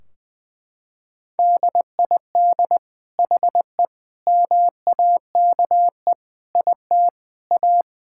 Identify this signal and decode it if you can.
Morse code, heard as a keyed tone, about 19 wpm, 700 Hz. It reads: DID HE MAKE IT A
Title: Sentences from Top 1600 Words 20wpm